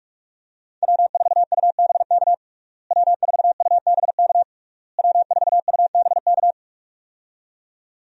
Morse code, heard as a keyed tone, about 41 wpm, 700 Hz. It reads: W4UBK W4UBK W4UBK
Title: USA Call Signs - 5 Characters - Repeated 3x - 45wpm